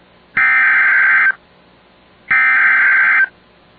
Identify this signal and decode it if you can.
Emergency Alert System (SAME) tones (header bursts), heard as harsh buzzing bursts. It is heard from 0.4 to 3.2 s.